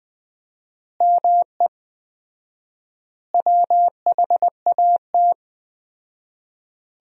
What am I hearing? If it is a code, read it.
Morse code, heard as a keyed tone, about 20 wpm, 700 Hz. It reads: ME WHAT
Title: 4x Word-Spacing - Sentences from Top 500 Words - 20wpm